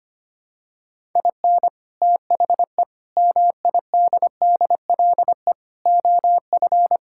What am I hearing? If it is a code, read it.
Morse code, heard as a keyed tone, about 25 wpm, 700 Hz. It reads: IN THE MIDDLE OF
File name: practice / 25wpm / Sentences with 4-Word Phrases - 25wpm